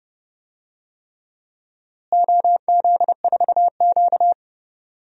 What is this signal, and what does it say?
Morse code, heard as a keyed tone, about 30 wpm, 700 Hz. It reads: OZ4Q
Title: DXCC-100 Call Signs - 4 Characters - Repeated 3x - 30wpm